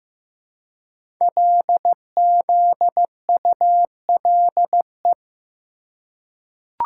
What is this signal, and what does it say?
Morse code, heard as a keyed tone, about 15 wpm, 700 Hz. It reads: LZULE
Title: Code Groups - 5 Letters - 15wpm